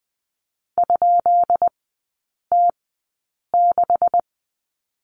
Morse code, heard as a keyed tone, about 20 wpm, spaced as for 8 wpm, 690 Hz. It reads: ?T6
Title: 159 Lesson 26e - Letter g - Three Characters 20wpm